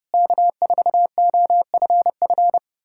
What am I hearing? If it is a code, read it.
Morse code, heard as a keyed tone, about 33 wpm, 700 Hz. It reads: K4OFF